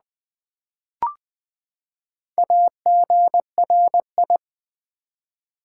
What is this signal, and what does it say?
Morse code, heard as a keyed tone, about 20 wpm, 700 Hz. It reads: AGRI
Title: Build A Word - 4 Letters At A Time - 11-Letter Words - 20wpm